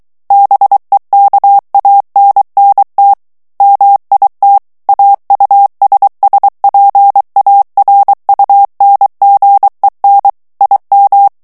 Morse code, heard as a keyed tone, about 23 wpm, 800 Hz. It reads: BEKANNT MIT AUSSPARUNGEN IM
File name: morse mp3 / ALLEMAND texte clair 1 vitesse 22 mots minute